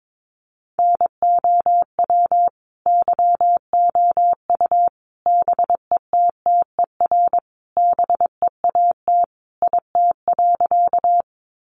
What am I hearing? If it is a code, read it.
Morse code, heard as a keyed tone, about 22 wpm, 700 Hz. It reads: NOW YOU BETTER BEAT IT.
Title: Sentences from Top 1300 Words 22wpm